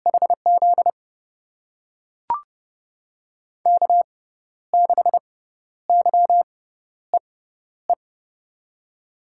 Morse code, heard as a keyed tone, about 30 wpm, spaced as for 12 wpm, 700 Hz. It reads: HZ K6YEE
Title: USA Call Signs - 5 Characters - 6x Character Spacing - 30wpm